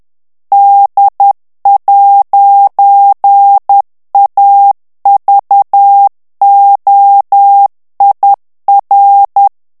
Morse code, heard as a keyed tone, about 10 wpm, 800 Hz. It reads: D'AVOIR
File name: CW / FICHIERS-MP3 / FRANCAIS TEXTE CLAIR 2 VITESSE 10 MOTS MINUTES